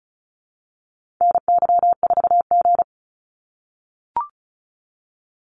Morse code, heard as a keyed tone, about 34 wpm, 700 Hz. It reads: NY4G
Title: NA SOTA Activator Callsigns 35wpm